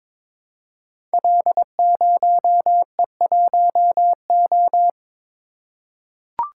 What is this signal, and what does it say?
Morse code, heard as a keyed tone, about 22 wpm, 700 Hz. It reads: L0E1O
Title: Code Groups - 5 Letter:Numbers - 22wpm